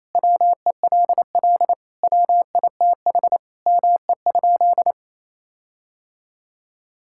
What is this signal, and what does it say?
Morse code, heard as a keyed tone, about 28 wpm, 700 Hz. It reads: WELL WITH ME?
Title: Sentences from Top 1600 Words 28wpm